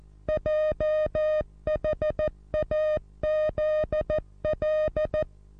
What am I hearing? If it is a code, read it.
Morse code, heard as a keyed tone, about 14 wpm, 620 Hz. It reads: JHAZL
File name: repos / Morsekurs / DJ4UF Morsen27